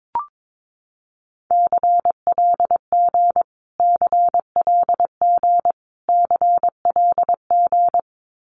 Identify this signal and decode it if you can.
Morse code, heard as a keyed tone, about 22 wpm, 700 Hz. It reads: CLG CLG CLG
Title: Abbreviations Repeated 3x - 22wpm